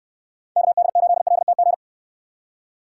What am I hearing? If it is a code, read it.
Morse code, heard as a keyed tone, about 69 wpm, 700 Hz. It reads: KD8LTG